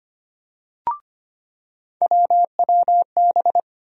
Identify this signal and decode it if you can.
Morse code, heard as a keyed tone, about 25 wpm, 700 Hz. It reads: WWB
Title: USA Call Sign Postfixes - 3 Letters 25wpm